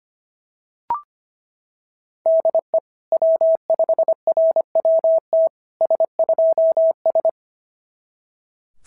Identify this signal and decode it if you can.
Morse code, heard as a keyed tone, about 25 wpm, 690 Hz. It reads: DE W5RWT S2S
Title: SOTA QSO Practice 25wpm